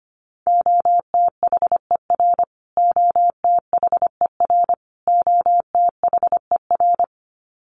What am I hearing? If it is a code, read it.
Morse code, heard as a keyed tone, about 25 wpm, 700 Hz. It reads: OTHER OTHER OTHER
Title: Top 500 Words Repeated 25wpm